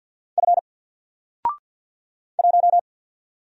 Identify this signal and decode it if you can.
Morse code, heard as a keyed tone, about 50 wpm, 700 Hz. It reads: F 1
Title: Single Letter-Number Alt 50wpm